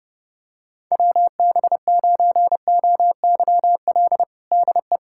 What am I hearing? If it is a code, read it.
Morse code, heard as a keyed tone, about 30 wpm, 700 Hz. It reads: WB9OYL DE
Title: USA Callsign DE Callsign - 6 Characters - 30wpm